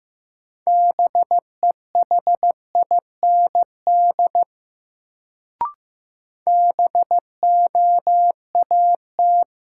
Morse code, heard as a keyed tone, about 15 wpm, 700 Hz. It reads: BEHIND BOAT
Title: Sentences from Top 500 Words - New Word Drill - 15wpm